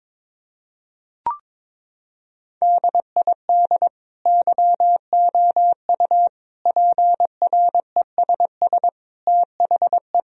Morse code, heard as a keyed tone, about 22 wpm, 700 Hz. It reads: DID YOU PRESS THE
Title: Sentences from Top 1000 Words - Review Part 1 - 22wpm